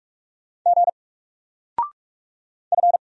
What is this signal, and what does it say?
Morse code, heard as a keyed tone, about 44 wpm, 700 Hz. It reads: G F
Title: Single Letter-Number 45wpm